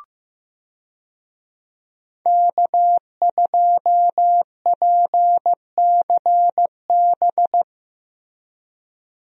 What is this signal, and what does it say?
Morse code, heard as a keyed tone, about 15 wpm, 700 Hz. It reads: K2PCB
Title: USA Call Signs - 5 Characters - Repeated 2x - 15wpm